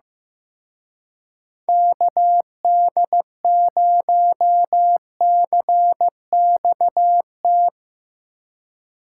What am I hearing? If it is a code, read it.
Morse code, heard as a keyed tone, about 16 wpm, 700 Hz. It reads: KD0CXT
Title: Speed Racing - USA 2x3 Call Signs 15wpm